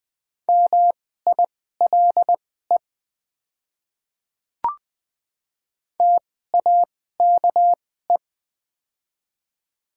Morse code, heard as a keyed tone, about 20 wpm, spaced as for 14 wpm, 700 Hz. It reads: MILE TAKE
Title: 2x Character Spacing - Top 300 Words - 20wpm